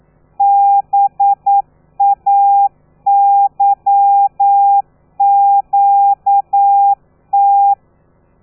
Morse code, heard as a keyed tone, about 9 wpm, 800 Hz. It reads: BAYQT
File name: telegrafia / i2tfj / Lez13 40